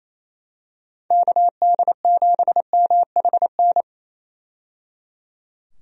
Morse code, heard as a keyed tone, about 28 wpm, 700 Hz. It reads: KD7MHN
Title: USA Call Signs - 6 Characters - 28wpm